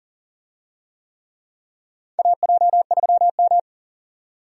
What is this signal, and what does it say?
Morse code, heard as a keyed tone, about 40 wpm, 700 Hz. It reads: AJ3M
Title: USA Call Signs - 4 Characters - 40wpm